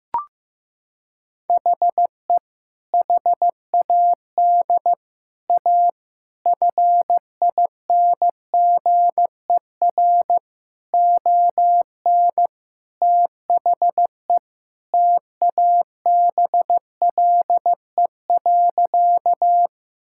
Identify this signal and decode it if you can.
Morse code, heard as a keyed tone, about 14 wpm, 700 Hz. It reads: HE HAD A FINGER ON THE TABLE.
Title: Sentences from Top 1600 Words 15wpm